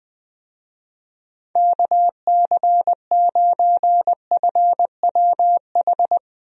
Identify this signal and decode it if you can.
Morse code, heard as a keyed tone, about 20 wpm, 700 Hz. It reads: KC9FWH
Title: USA Call Signs - 6 Characters - Repeated 2x - 20wpm